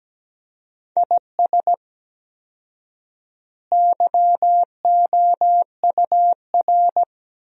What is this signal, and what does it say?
Morse code, heard as a keyed tone, about 17 wpm, 700 Hz. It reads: IS YOUR
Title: 4x Word-Spacing - Sentences from Top 400 Words - 17wpm